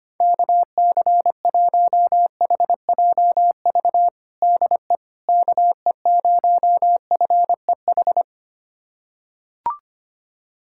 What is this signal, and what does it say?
Morse code, heard as a keyed tone, about 25 wpm, 700 Hz. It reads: KC1HJV DE KE0FEH
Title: USA Callsign DE Callsign 25wpm